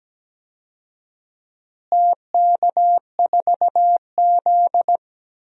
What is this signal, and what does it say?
Morse code, heard as a keyed tone, about 17 wpm, 700 Hz. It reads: TK4Z
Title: DXCC-100 Call Signs - 4 Characters - Repeated 2x - 17wpm